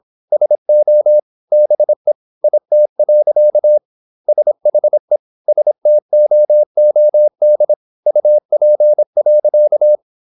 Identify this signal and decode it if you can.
Morse code, heard as a keyed tone, about 26 wpm, 600 Hz. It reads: SO BE IT. SHE STOOD UP.
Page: Never 26wpm | CW med Gnister